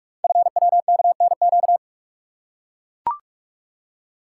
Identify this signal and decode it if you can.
Morse code, heard as a keyed tone, about 44 wpm, 700 Hz. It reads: FWKNQ
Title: Code Groups - 5 Letters - 45wpm